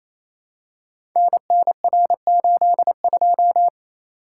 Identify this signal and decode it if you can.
Morse code, heard as a keyed tone, about 29 wpm, 700 Hz. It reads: NNR82
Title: Code Groups - 5 Letter:Numbers - 28wpm